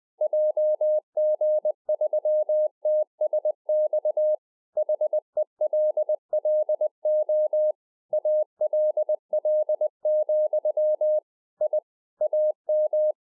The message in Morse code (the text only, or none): JG3TSX HELLO ALL, I AM